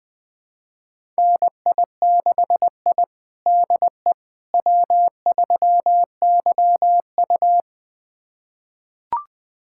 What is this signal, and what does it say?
Morse code, heard as a keyed tone, about 20 wpm, 700 Hz. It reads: NI6I DE W3YU
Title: USA Callsign DE Callsign - 4 Characters - 20wpm